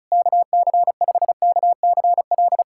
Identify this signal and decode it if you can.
Morse code, heard as a keyed tone, about 36 wpm, 700 Hz. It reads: KC5KCL
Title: USA Callsign DE Callsign 35wpm